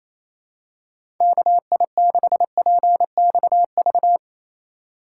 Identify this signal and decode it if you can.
Morse code, heard as a keyed tone, about 28 wpm, 700 Hz. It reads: KI6PXV
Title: USA Call Signs - 6 Characters - 8x Character Spacing - 28wpm